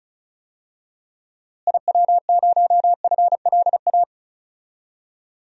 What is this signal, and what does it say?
Morse code, heard as a keyed tone, about 36 wpm, 700 Hz. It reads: IW0FLA
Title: DXCC-100 Call Signs - 6 Characters - Repeated 3x - 35wpm